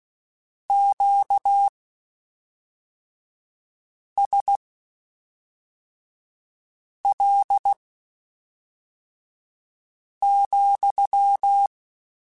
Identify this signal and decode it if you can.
Morse code, heard as a keyed tone, about 16 wpm, spaced as for 3 wpm, 800 Hz. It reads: QSL,